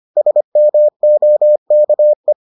Morse code, heard as a keyed tone, about 25 wpm, 600 Hz. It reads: SMOKE